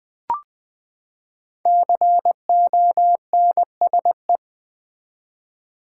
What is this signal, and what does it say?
Morse code, heard as a keyed tone, about 21 wpm, 700 Hz. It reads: CONSE
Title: Build A Word - 5 Letters At A Time - 12-Letter Words - 20wpm